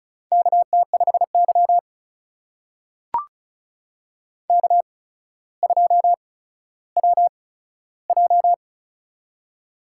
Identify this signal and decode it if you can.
Morse code, heard as a keyed tone, about 34 wpm, spaced as for 15 wpm, 700 Hz. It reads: KT5Y K2WJ
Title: USA Call Signs - 4 Characters - 8x Character Spacing - 35wpm